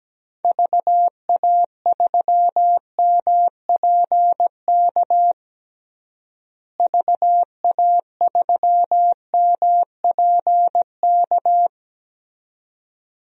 Morse code, heard as a keyed tone, about 17 wpm, 700 Hz. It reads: VA3MPK VA3MPK
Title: DXCC-100 Call Signs - 6 Characters - Repeated 2x - 17wpm